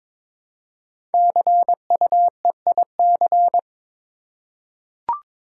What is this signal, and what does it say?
Morse code, heard as a keyed tone, about 22 wpm, 700 Hz. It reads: CUEIC
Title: Code Groups - 5 Letter:Numbers - 22wpm